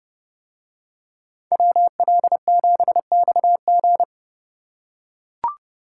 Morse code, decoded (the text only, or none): WL7XG